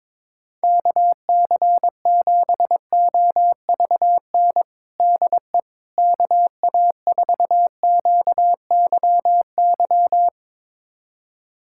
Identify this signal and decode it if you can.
Morse code, heard as a keyed tone, about 22 wpm, 700 Hz. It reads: KC7OVN DE KA4QYY
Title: USA Callsign DE Callsign 22wpm